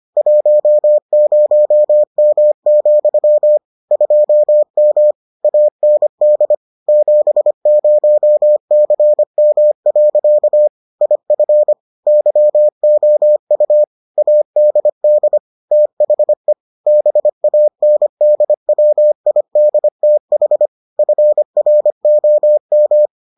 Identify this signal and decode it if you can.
Morse code, heard as a keyed tone, about 25 wpm, 600 Hz. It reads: 10M, 2M AND 70CM. IF YOU ADD THE BANDWIDTH FROM